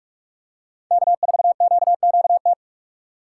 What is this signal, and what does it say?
Morse code, heard as a keyed tone, about 46 wpm, 700 Hz. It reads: K4QQT
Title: USA Call Signs - 5 Characters - 6x Character Spacing - 45wpm